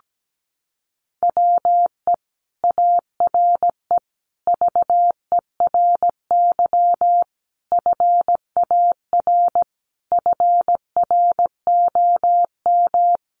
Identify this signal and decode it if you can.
Morse code, heard as a keyed tone, about 17 wpm, 700 Hz. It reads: WE ARE VERY FAR FROM
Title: Sentences from Top 300 Words 17wpm